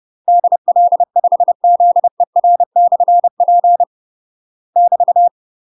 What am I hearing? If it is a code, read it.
Morse code, heard as a keyed tone, about 30 wpm, 700 Hz. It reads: DL5ZER/P <BT>
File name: UK MASTER QSOs TRAINING / 30WPM / UK MASTER QSOs 0004